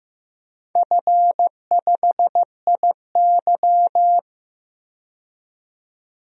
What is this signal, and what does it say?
Morse code, heard as a keyed tone, about 15 wpm, 700 Hz. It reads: F5IY
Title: DXCC-100 Call Signs - 4 Characters - 15wpm